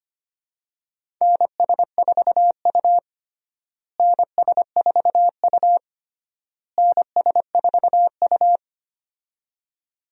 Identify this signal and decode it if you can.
Morse code, heard as a keyed tone, about 25 wpm, 700 Hz. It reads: NS4U NS4U NS4U
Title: USA Call Signs - 4 Characters - Repeated 3x - 25wpm